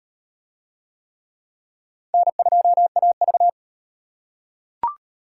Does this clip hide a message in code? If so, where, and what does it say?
Yes: Morse code from 2.1 to 3.5 s, heard as a keyed tone, about 38 wpm, 700 Hz. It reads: N2AV